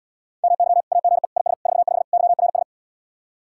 Morse code, heard as a keyed tone, about 77 wpm, 700 Hz. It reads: MY AGE IS 45 YRS